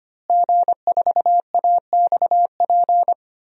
Morse code, heard as a keyed tone, about 25 wpm, 700 Hz. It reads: G4AXP
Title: DXCC-100 Call Signs - 5 Characters - Repeated 2x - 25wpm